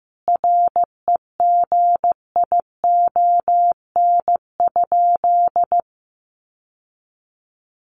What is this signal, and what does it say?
Morse code, heard as a keyed tone, about 15 wpm, 700 Hz. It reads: REGION?
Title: Sentences from Top 1500 Words 15wpm